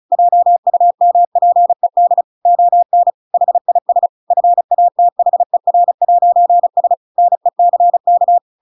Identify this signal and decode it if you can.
Morse code, heard as a keyed tone, about 35 wpm, 700 Hz. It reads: JUMPED ON HIS FATHER'S NECK
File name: Pinocchio / Pinocchio ch8 35 35 ik1hge